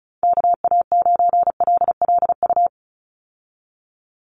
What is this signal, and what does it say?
Morse code, heard as a keyed tone, about 34 wpm, 700 Hz. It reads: KA9LLU
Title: USA Call Signs 35wpm